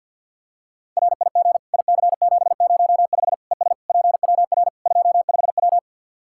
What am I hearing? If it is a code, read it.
Morse code, heard as a keyed tone, about 51 wpm, 700 Hz. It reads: RIG IC705 ES PWR 25W
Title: Practice QSO - Exchange 2 - Info - 50wpm